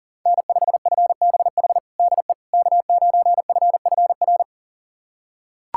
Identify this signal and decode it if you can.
Morse code, heard as a keyed tone, about 40 wpm, 700 Hz. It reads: N5FBH DE K9FFR
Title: USA Callsign DE Callsign - 5 Characters - 40wpm